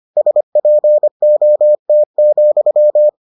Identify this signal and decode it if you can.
Morse code, heard as a keyed tone, about 25 wpm, 600 Hz. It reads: SPOT,